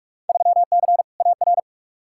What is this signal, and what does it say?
Morse code, heard as a keyed tone, about 47 wpm, 700 Hz. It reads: 3C AR
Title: Field Day Exchanges Encore - 45wpm